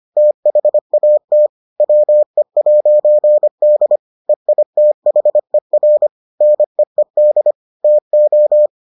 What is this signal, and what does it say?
Morse code, heard as a keyed tone, about 25 wpm, 600 Hz. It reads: THAT WE'D EITHER NEED TO